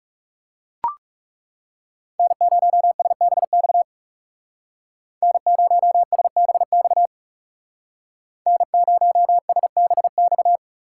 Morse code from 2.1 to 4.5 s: N0SBX

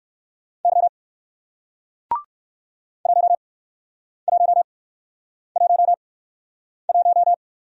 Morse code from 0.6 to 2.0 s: Y

